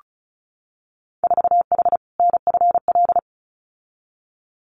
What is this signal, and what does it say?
Morse code, heard as a keyed tone, about 36 wpm, 700 Hz. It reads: 4H NFL
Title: Winter Field Day Exchanges Encore 35wpm